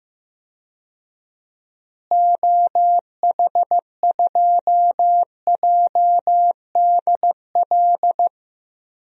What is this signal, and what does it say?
Morse code, heard as a keyed tone, about 16 wpm, 700 Hz. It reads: OH2JDL
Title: DXCC-100 Call Signs - 6 Characters - Repeated 2x - 15wpm